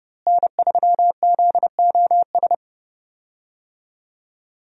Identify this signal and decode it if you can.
Morse code, heard as a keyed tone, about 30 wpm, 700 Hz. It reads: N3ZOS
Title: USA Call Signs - 5 Characters - Repeated 2x - 30wpm